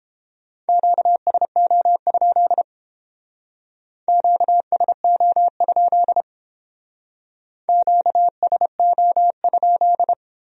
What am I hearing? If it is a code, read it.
Morse code, heard as a keyed tone, about 29 wpm, 700 Hz. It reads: QSO? QSO? QSO?